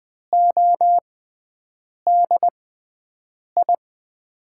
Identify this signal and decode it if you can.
Morse code, heard as a keyed tone, about 20 wpm, spaced as for 7 wpm, 700 Hz. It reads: ODI
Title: USA Call Signs - 5 Characters - 6x Character Spacing - 20wpm